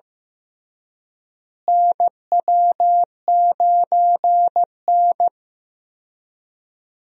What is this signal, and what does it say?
Morse code, heard as a keyed tone, about 15 wpm, 700 Hz. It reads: NW9N
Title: USA Call Signs - 4 Characters - Repeated 3x - 15wpm